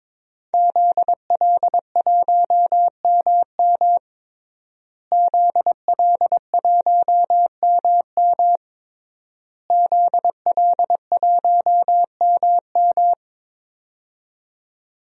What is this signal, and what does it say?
Morse code, heard as a keyed tone, about 22 wpm, 700 Hz. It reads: ZL1MM ZL1MM ZL1MM
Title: DXCC-100 Call Signs - 5 Characters - Repeated 3x - 22wpm